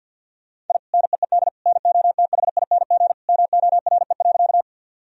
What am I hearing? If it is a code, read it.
Morse code, heard as a keyed tone, about 52 wpm, 700 Hz. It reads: I NEED NOTHING MORE.
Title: Sentences from Top 1000 Words - Review Part 1 - 50wpm